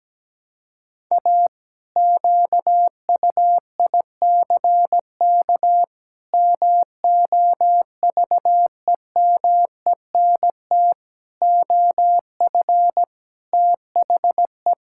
Morse code, heard as a keyed tone, about 17 wpm, 700 Hz. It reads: A QUICK MOVEMENT OF THE
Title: Pangram Challenge v2 - 17wpm